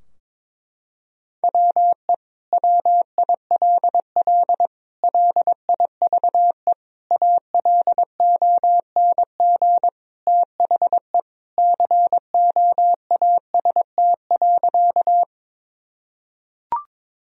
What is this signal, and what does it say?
Morse code, heard as a keyed tone, about 22 wpm, 700 Hz. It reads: WE WILL LIVE ALONG THE COAST.
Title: Sentences from Top 1000 Words - Review Part 4 - 22wpm